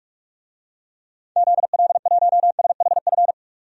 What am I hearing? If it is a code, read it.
Morse code, heard as a keyed tone, about 46 wpm, 700 Hz. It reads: ZL1SHF
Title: DXCC-100 Call Signs - 6 Characters - 45wpm